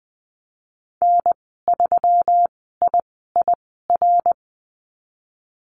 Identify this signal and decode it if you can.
Morse code, heard as a keyed tone, about 19 wpm, spaced as for 14 wpm, 700 Hz. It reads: N3IIR
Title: 147 Lesson 24g - Letter y - Call Signs 20wpm